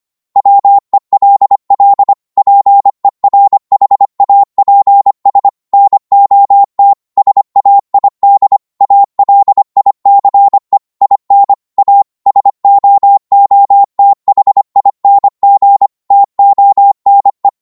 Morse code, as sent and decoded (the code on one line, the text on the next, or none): .-- . .-.. .-.. / .--. .-. .... .- .--. ... / -. --- - / ... .- .. -.. / .- .-.. .. -.-. / .. -. / .- / ... --- --- - .... .. -. --. / - --- -. .
WELL PERHAPS NOT SAID ALICE IN A SOOTHING TONE